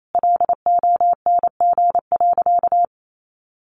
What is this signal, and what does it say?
Morse code, heard as a keyed tone, about 28 wpm, 700 Hz. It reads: LONG.
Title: Sentences from Top 700 Words 28wpm